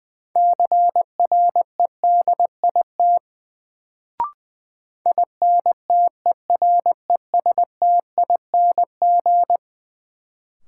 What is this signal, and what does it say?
Morse code, heard as a keyed tone, about 20 wpm, 700 Hz. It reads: CREDIT INTERESTING
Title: Sentences from Top 1000 Words - New Word Drill - 20wpm